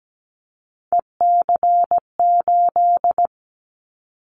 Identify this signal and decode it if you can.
Morse code, heard as a keyed tone, about 17 wpm, 700 Hz. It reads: EC8
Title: Callsign Prefixes for Top 50 DXCC - 17wpm